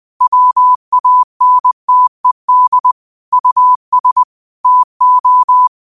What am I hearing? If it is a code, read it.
Morse code, heard as a keyed tone, about 20 wpm, 1000 Hz. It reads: WANTED US TO
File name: qotdfiles / 20wpm